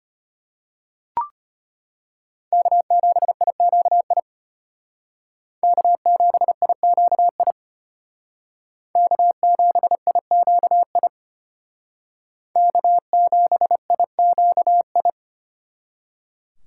Morse code, heard as a keyed tone, about 32 wpm, 700 Hz. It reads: K7IQI K7IQI K7IQI K7IQI